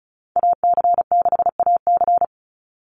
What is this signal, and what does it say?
Morse code, heard as a keyed tone, about 34 wpm, 700 Hz. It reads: AC6AC
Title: Call Signs - 35wpm